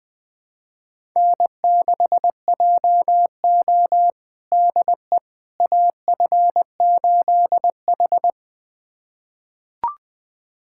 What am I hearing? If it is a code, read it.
Morse code, heard as a keyed tone, about 21 wpm, 700 Hz. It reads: N6JO DE AF8H
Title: USA Callsign DE Callsign - 4 Characters - 20wpm